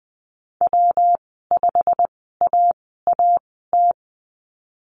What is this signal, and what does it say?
Morse code, heard as a keyed tone, about 20 wpm, spaced as for 15 wpm, 700 Hz. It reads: W5AAT
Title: 154 Lesson 25g - Letter p - Call Signs 20wpm